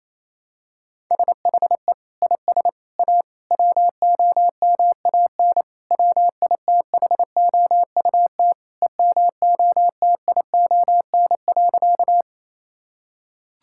Morse code, heard as a keyed tone, about 28 wpm, 700 Hz. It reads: SHE IS A WOMAN WITHOUT EMOTION.
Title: Sentences from Top 1600 Words 28wpm